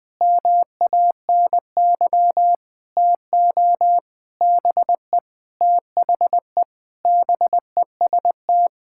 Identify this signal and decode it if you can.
Morse code, heard as a keyed tone, about 21 wpm, 700 Hz. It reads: MANY TO BE THE BEST
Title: Sentences with 7-Word Phrases - 20wpm